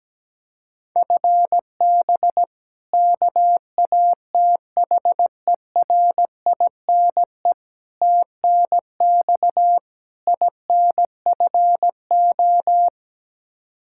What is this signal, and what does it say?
Morse code, heard as a keyed tone, about 17 wpm, 700 Hz. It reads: FB KATHERINE TNX INFO
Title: Practice QSO - Exchange 3 - More Info - 17wpm